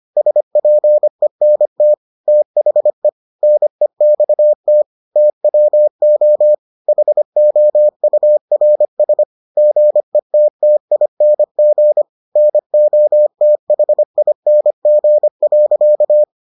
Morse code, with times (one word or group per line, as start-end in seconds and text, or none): 0.2-1.9 SPENT
2.3-3.1 THE
3.4-4.8 NEXT
5.2-6.5 TWO
6.9-9.2 HOURS
9.6-12.0 GETTING
12.4-16.2 NOTHING.